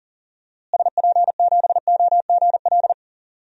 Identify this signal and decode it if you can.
Morse code, heard as a keyed tone, about 39 wpm, 700 Hz. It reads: SP7OGL
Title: DXCC-100 Call Signs - 6 Characters - Repeated 2x - 40wpm